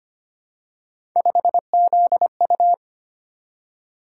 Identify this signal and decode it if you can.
Morse code, heard as a keyed tone, about 25 wpm, 700 Hz. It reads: 5ZU
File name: Three Letter-Number Combinations - 25wpm